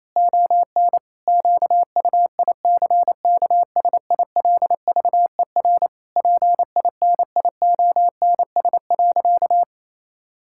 Morse code, heard as a keyed tone, about 27 wpm, 700 Hz. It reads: ON QUICKSILVER PINIONS.
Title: Pangram Challenge v2 - 28wpm